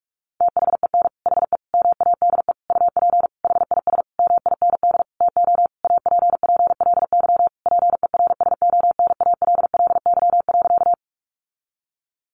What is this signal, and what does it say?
Morse code, heard as a keyed tone, about 44 wpm, 700 Hz. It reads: THEN HE MADE UP HIS MIND TO APPLY PERSONALLY.
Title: Sentences from Top 1400 Words 45wpm